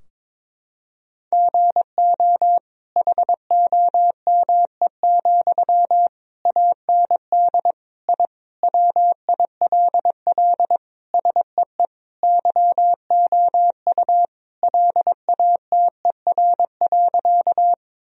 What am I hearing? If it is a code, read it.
Morse code, heard as a keyed tone, about 22 wpm, 700 Hz. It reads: GO HOME, AND I WILL SEE YOU LATER.
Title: Sentences from Top 1000 Words - Review Part 4 - 22wpm